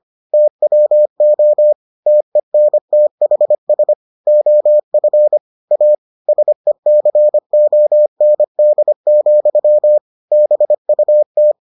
Morse code, heard as a keyed tone, about 25 wpm, 600 Hz. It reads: TWO TENTHS OF A SECOND, BUT